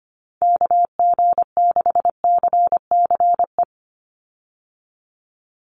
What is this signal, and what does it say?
Morse code, heard as a keyed tone, about 25 wpm, 700 Hz. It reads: KG6CCE
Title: USA Call Signs 25wpm